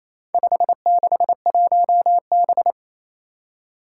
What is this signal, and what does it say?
Morse code, heard as a keyed tone, about 27 wpm, 700 Hz. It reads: 561B
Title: Four Challenging Letter-Number Combinations 28wpm